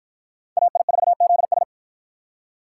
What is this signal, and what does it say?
Morse code, heard as a keyed tone, about 51 wpm, 700 Hz. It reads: AI4ZS